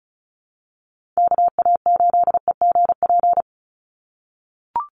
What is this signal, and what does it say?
Morse code, heard as a keyed tone, about 34 wpm, 700 Hz. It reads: KA8EGP
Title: USA Call Signs 35wpm